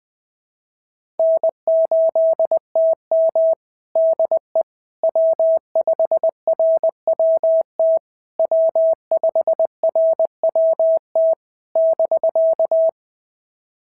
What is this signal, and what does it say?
Morse code, heard as a keyed tone, about 19 wpm, 650 Hz. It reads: N8TM DE W5RWT W5RWT <BK>
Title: SOTA QSO Practice 20wpm